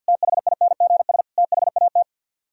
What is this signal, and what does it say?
Morse code, heard as a keyed tone, about 64 wpm, 700 Hz. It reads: THINGS THAT